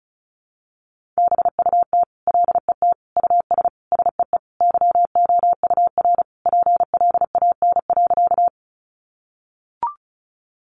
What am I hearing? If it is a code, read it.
Morse code, heard as a keyed tone, about 34 wpm, 700 Hz. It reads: BUT LET US SEE YOUR PLAN.
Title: Sentences from Top 500 Words 35wpm